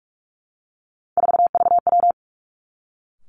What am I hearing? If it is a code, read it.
Morse code, heard as a keyed tone, about 46 wpm, 700 Hz. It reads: 4VW
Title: Three Challenging Letter-Number Combinations 45wpm